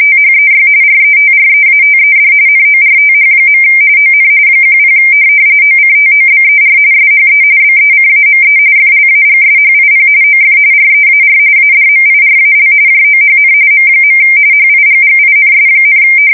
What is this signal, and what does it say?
RTTY, about 45.45 baud. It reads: THEY RETREATED NORTH. ON 10 OCTOBER SHE BECAME FLAGSHIP OF REAR ADMIRAL JOHN M. HIGGINS,
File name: fsk170 / USS-Missouri